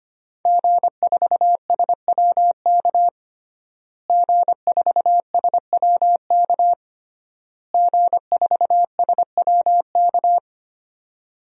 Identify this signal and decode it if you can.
Morse code, heard as a keyed tone, about 25 wpm, 700 Hz. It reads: G4SWK G4SWK G4SWK
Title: DXCC-100 Call Signs - 5 Characters - Repeated 3x - 25wpm